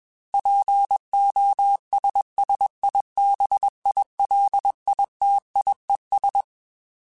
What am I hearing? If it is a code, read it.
Morse code, heard as a keyed tone, about 21 wpm, 800 Hz. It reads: POSSIBILITIES
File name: Technical Subjects / Technical Subjects-mp3s / OR Gate @20